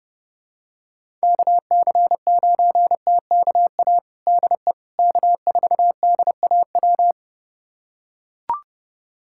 Morse code, heard as a keyed tone, about 30 wpm, 700 Hz. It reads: KC9TKA DE K4DAW
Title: USA Callsign DE Callsign 30wpm